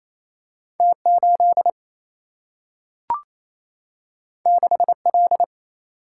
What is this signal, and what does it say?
Morse code, heard as a keyed tone, about 28 wpm, 700 Hz. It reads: T8 6L
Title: Single-Letter Single-Number 28wpm